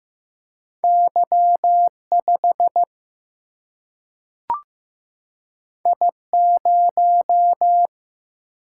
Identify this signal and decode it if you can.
Morse code, heard as a keyed tone, about 15 wpm, 700 Hz. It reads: Y 5 I 0
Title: Single-Letter Single-Number 15wpm